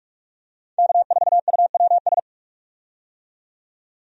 Morse code, heard as a keyed tone, about 47 wpm, 700 Hz. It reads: K4UWS
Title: USA Call Signs - 5 Characters - 45wpm